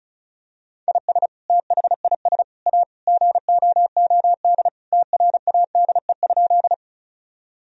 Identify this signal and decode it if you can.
Morse code, heard as a keyed tone, about 36 wpm, 700 Hz. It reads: IS THIS A GOOD TRADE?
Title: Sentences from Top 1000 Words - Review Part 4 - 35wpm